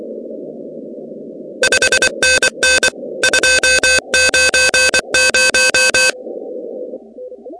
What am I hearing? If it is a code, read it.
Morse code, heard as a keyed tone, about 24 wpm, 1550 Hz, with noise. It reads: 5NN 290
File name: contests / 2019-cw-wpx-kt5j-k5pi- -2330 0-16